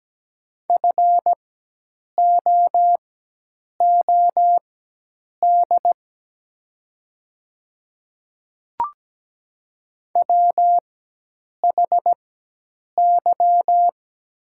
Morse code, heard as a keyed tone, about 17 wpm, spaced as for 8 wpm, 700 Hz. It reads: FOOD WHY
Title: 4x Character Spacing - Top 300 Words - 17wpm